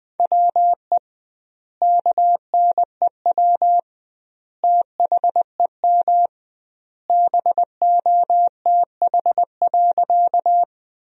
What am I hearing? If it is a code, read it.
Morse code, heard as a keyed tone, about 21 wpm, 700 Hz. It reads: WE KNEW THEM BOTH.
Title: 2x Word-Spacing - Sentences from Top 400 Words - 20wpm